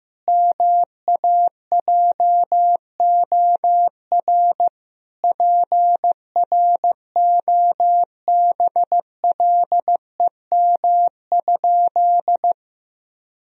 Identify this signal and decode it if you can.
Morse code, heard as a keyed tone, about 15 wpm, 700 Hz. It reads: MAJOR PROBLEM?
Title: Sentences from Top 1000 Words - Review Part 4 - 15wpm